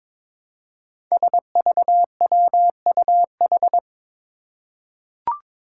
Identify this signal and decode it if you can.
Morse code, heard as a keyed tone, about 22 wpm, 700 Hz. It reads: SVWUH
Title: Five Challenging Letter-Number Combinations 22wpm